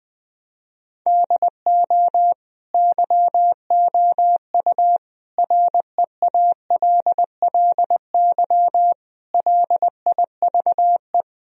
Morse code, heard as a keyed tone, about 20 wpm, 700 Hz. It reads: DO YOU REALLY LIVE
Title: Sentences from Top 1000 Words - Review Part 1 - 20wpm